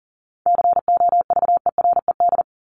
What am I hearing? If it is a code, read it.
Morse code, heard as a keyed tone, about 40 wpm, 700 Hz. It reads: COVERED
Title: ED Suffix - Top 100 Words with Base Words 40wpm